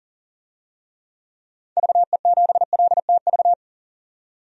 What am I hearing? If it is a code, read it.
Morse code, heard as a keyed tone, about 40 wpm, 700 Hz. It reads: VE7LTV
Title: DXCC-100 Call Signs - 6 Characters - 40wpm